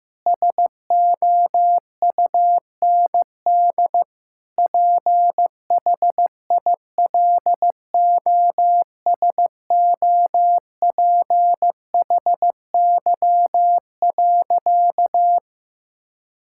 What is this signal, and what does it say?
Morse code, heard as a keyed tone, about 16 wpm, 700 Hz. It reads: SOUND PHILOSOPHY.
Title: Sentences from Top 1000 Words - Review Part 6 - 15wpm